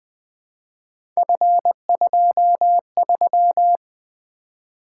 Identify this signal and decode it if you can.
Morse code, heard as a keyed tone, about 20 wpm, 700 Hz. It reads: F23
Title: Three Letter-Number Combinations - 20wpm